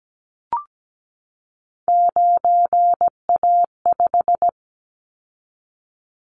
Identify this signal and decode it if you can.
Morse code, heard as a keyed tone, about 17 wpm, 700 Hz. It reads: 9A5
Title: Callsign Prefixes for Top 100 DXCC - 17wpm